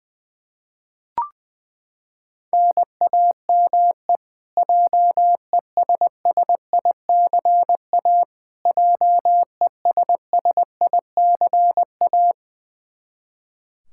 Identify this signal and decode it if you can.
Morse code, heard as a keyed tone, about 20 wpm, 700 Hz. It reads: NAME JESSICA JESSICA
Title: Practice QSO - Exchange 1 - Beginning - 20wpm